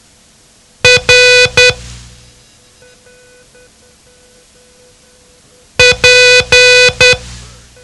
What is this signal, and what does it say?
Morse code, heard as a keyed tone, about 10 wpm, 500 Hz. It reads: R P
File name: ham radio / CodeQuick / Code Quick 07